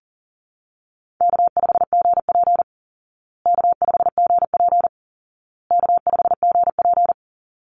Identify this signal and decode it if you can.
Morse code, heard as a keyed tone, about 40 wpm, 700 Hz. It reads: K5GP K5GP K5GP
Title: Easy Call Signs Repeated 3x - 40wpm